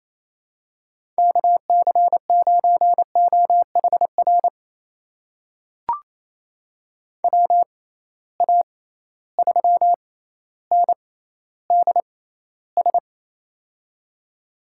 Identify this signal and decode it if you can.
Morse code, heard as a keyed tone, about 27 wpm, spaced as for 14 wpm, 700 Hz. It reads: KC9OHR WA3NDS
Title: USA Call Signs - 6 Characters - 6x Character Spacing - 28wpm